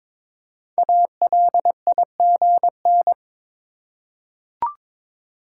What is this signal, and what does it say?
Morse code, heard as a keyed tone, about 22 wpm, 700 Hz. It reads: ALIGN
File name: 5 Letter Words - First 3 Letters Given - 22wpm